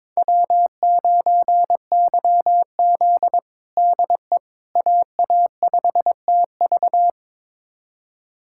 Morse code, heard as a keyed tone, about 22 wpm, 700 Hz. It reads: W9YZ DE AA5TV
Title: USA Callsign DE Callsign 22wpm